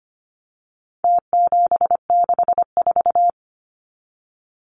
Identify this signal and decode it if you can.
Morse code, heard as a keyed tone, about 25 wpm, 700 Hz. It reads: T764
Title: Numbers - Contest Style - 4-Digit 25wpm